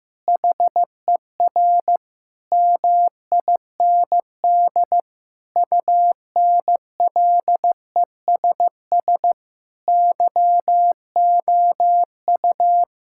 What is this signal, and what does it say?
Morse code, heard as a keyed tone, about 15 wpm, 700 Hz. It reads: HER MIND UNLESS YOU
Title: Sentences from Top 1000 Words - Review Part 6 - 15wpm